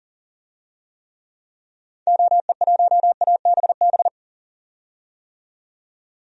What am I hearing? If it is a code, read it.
Morse code, heard as a keyed tone, about 40 wpm, 700 Hz. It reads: OE1ABB
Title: DXCC-100 Call Signs - 6 Characters - 40wpm